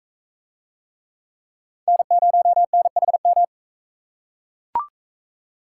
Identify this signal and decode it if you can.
Morse code, heard as a keyed tone, about 41 wpm, 700 Hz. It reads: N0NHM